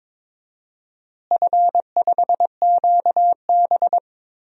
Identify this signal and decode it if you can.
Morse code, heard as a keyed tone, about 22 wpm, 700 Hz. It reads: F5QB
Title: DXCC-100 Call Signs - 4 Characters - Repeated 3x - 22wpm